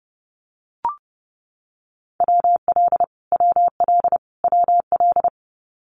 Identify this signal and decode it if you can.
Morse code, heard as a keyed tone, about 30 wpm, 700 Hz. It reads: WL WL WL
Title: Abbreviations Repeated 3x - 30wpm